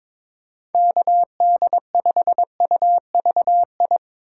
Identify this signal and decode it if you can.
Morse code, heard as a keyed tone, about 22 wpm, 700 Hz. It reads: KD5UVI
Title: USA Call Signs - 6 Characters - 3x Character Spacing - 22wpm